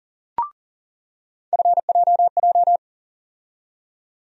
Morse code, heard as a keyed tone, about 40 wpm, 700 Hz. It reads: FJJ
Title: USA Call Sign Postfixes - 3 Letters 40wpm